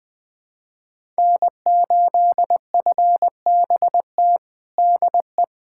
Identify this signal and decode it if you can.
Morse code, heard as a keyed tone, about 20 wpm, 700 Hz. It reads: N8FBT DE
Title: USA Callsign DE Callsign - 5 Characters - 20wpm